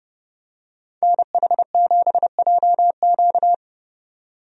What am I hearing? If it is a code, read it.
Morse code, heard as a keyed tone, about 32 wpm, 700 Hz. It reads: NH7JQ
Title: USA Call Signs - 5 Characters - 30wpm